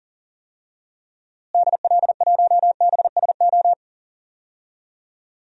Morse code, heard as a keyed tone, about 39 wpm, 700 Hz. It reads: DL1BSO
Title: DXCC-100 Call Signs - 6 Characters - Repeated 2x - 40wpm